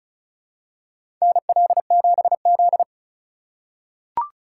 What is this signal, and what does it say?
Morse code, heard as a keyed tone, about 34 wpm, 700 Hz. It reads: NL7Z
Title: USA Call Signs - 4 Characters - 2x Character Spacing - 35wpm